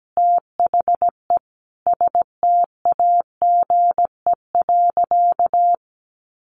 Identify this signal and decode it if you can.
Morse code, heard as a keyed tone, about 17 wpm, 700 Hz. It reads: THE STAGE.
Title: Sentences from Top 1400 Words 17wpm